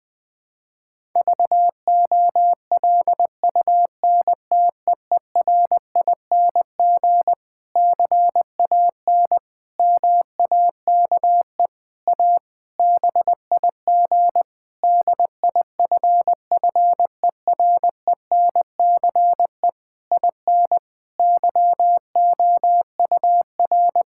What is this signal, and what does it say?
Morse code, heard as a keyed tone, about 20 wpm, 700 Hz. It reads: VOLUNTEERING CAN MAKE A BIG DIFFERENCE IN YOUR
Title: Difficult Sentences with Hints - 20wpm